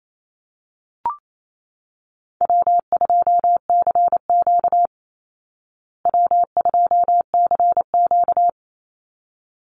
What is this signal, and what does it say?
Morse code, heard as a keyed tone, about 28 wpm, 700 Hz. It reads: W2CQ W2CQ
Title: Call Signs Repeated 3x - 28wpm